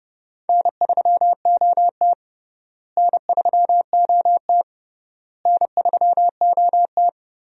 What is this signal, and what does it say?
Morse code, heard as a keyed tone, about 32 wpm, 700 Hz. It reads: N3OT N3OT N3OT
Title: USA Call Signs - 4 Characters - Repeated 3x - 30wpm